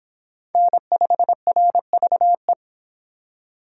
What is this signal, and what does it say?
Morse code, heard as a keyed tone, about 26 wpm, 700 Hz. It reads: N5RVE